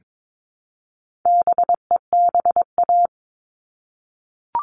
Morse code, heard as a keyed tone, about 22 wpm, 700 Hz. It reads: BEBA
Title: All Cut Numbers - 4-Digit 22wpm